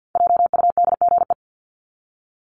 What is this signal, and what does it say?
Morse code, heard as a keyed tone, about 48 wpm, 700 Hz. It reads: JUDGE
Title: Top 1751-2000 Words 50wpm